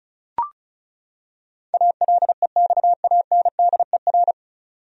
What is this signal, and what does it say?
Morse code, heard as a keyed tone, about 34 wpm, 700 Hz. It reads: ALEXANDER
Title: USA Common Names - 35wpm